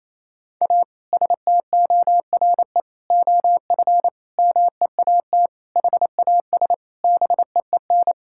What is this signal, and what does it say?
Morse code, heard as a keyed tone, about 27 wpm, 700 Hz. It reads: A STORE OF MEAT HAS BEEN
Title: Sentences from Top 1000 Words - Review Part 5 - 28wpm